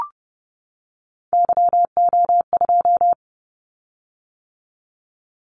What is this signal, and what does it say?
Morse code, heard as a keyed tone, about 30 wpm, 700 Hz. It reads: YO2
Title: Callsign Prefixes for Top 25 DXCC - 30wpm